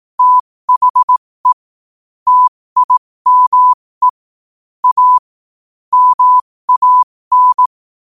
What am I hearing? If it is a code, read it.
Morse code, heard as a keyed tone, about 18 wpm, spaced as for 15 wpm, 1000 Hz. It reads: THE TIME A MAN
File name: qotdfiles / 15wpm